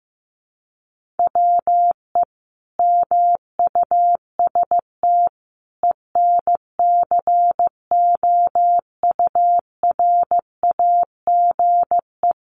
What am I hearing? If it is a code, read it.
Morse code, heard as a keyed tone, about 15 wpm, 700 Hz. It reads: WE MUST ENCOURAGE
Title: Sentences from Top 1400 Words 15wpm